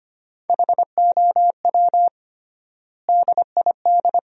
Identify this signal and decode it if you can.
Morse code, heard as a keyed tone, about 25 wpm, 700 Hz. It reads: HOW DID
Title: 3x Word-Spacing - Sentences from Top 200 Words - 25wpm